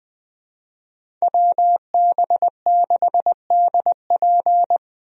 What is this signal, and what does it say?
Morse code, heard as a keyed tone, about 20 wpm, 700 Hz. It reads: WB6DP
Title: USA Call Signs - 5 Characters - Repeated 3x - 20wpm